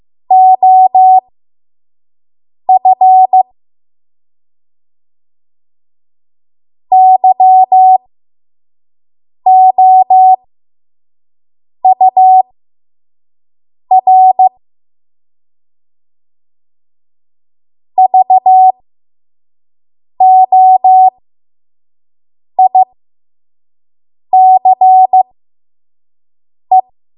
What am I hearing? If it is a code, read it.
Morse code, heard as a keyed tone, about 15 wpm, spaced as for 5 wpm, 750 Hz. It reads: OF YOUR VOICE
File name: Morse / 05WPM